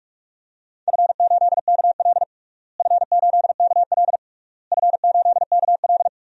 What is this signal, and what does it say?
Morse code, heard as a keyed tone, about 42 wpm, 700 Hz. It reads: F8KL F8KL F8KL
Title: DXCC-100 Call Signs - 4 Characters - Repeated 3x - 45wpm